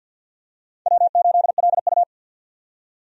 Morse code, heard as a keyed tone, about 53 wpm, 700 Hz. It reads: W8LU
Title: USA Call Signs - 4 Characters - 50wpm